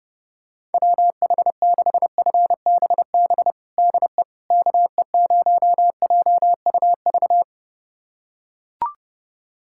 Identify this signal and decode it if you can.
Morse code, heard as a keyed tone, about 30 wpm, 700 Hz. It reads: WH6FBB DE KE0JUV
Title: USA Callsign DE Callsign 30wpm